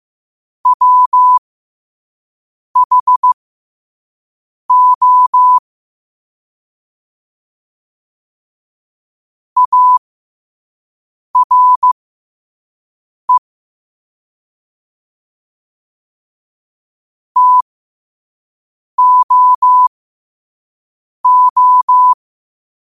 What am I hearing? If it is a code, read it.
Morse code, heard as a keyed tone, about 15 wpm, spaced as for 5 wpm, 1000 Hz. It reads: WHO ARE TOO
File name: qotdfiles / 5wpm